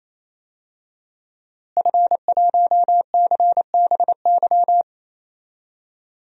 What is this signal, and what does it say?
Morse code, heard as a keyed tone, about 28 wpm, 700 Hz. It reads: F1CBY
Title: DXCC-100 Call Signs - 5 Characters - Repeated 3x - 28wpm